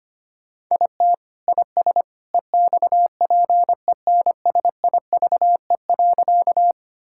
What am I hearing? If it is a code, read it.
Morse code, heard as a keyed tone, about 25 wpm, 700 Hz. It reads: IT IS EXPENSIVE.
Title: Sentences from Top 1000 Words - Review Part 6 - 25wpm